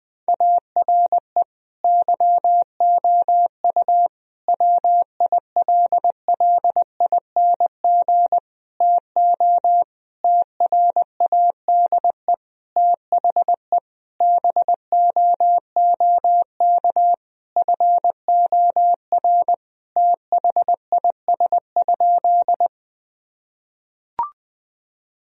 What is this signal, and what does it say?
Morse code, heard as a keyed tone, about 20 wpm, 700 Hz. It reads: ARE YOU WILLING TO TRADE THE BOOK FOR THIS?
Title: Sentences from Top 1000 Words - Review Part 4 - 20wpm